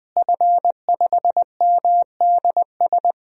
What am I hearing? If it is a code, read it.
Morse code, heard as a keyed tone, about 20 wpm, 700 Hz. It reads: F5MDS
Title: DXCC-100 Call Signs - 5 Characters - Repeated 3x - 20wpm